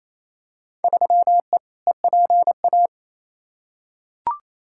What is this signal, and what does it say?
Morse code, heard as a keyed tone, about 28 wpm, 700 Hz. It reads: 3E EPA
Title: Field Day Exchanges Encore - 28wpm